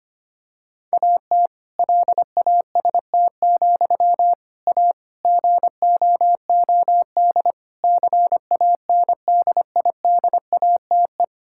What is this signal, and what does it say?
Morse code, heard as a keyed tone, about 25 wpm, 700 Hz. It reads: AT LAST, A GOOD CANDIDATE
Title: Sentences from Top 1600 Words 25wpm